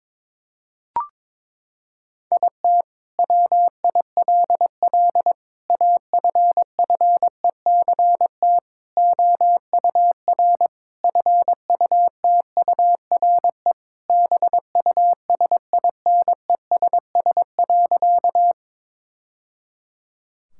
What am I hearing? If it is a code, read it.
Morse code, heard as a keyed tone, about 22 wpm, 700 Hz. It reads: IT WILL AFFECT OUR FUTURE BUSINESS.
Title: Sentences from Top 1000 Words - Review Part 6 - 22wpm